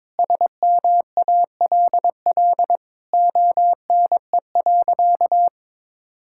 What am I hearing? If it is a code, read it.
Morse code, heard as a keyed tone, about 22 wpm, 700 Hz. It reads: SMALL ONE.
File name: practice / 22wpm / Sentences with 2-Word Phrases - 22wpm